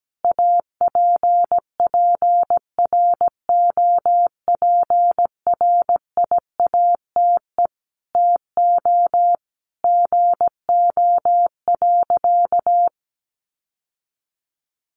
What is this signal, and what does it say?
Morse code, heard as a keyed tone, about 17 wpm, 700 Hz. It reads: APPROPRIATE TO GO.
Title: Sentences from Top 900 Words 17wpm